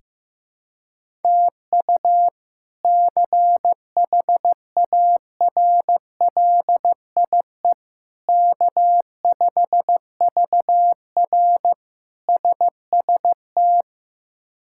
Morse code, heard as a keyed tone, about 15 wpm, 700 Hz. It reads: TU CHARLIE K5VR SST
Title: K1USN SST QSO Practice - 15wpm